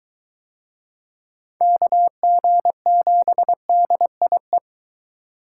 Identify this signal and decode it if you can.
Morse code, heard as a keyed tone, about 21 wpm, 700 Hz. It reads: KG7DIE